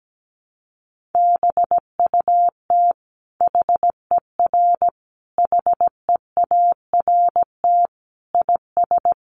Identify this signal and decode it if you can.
Morse code, heard as a keyed tone, about 17 wpm, 700 Hz. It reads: BUT HER HEART IS
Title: Sentences from Top 1200 Words 17wpm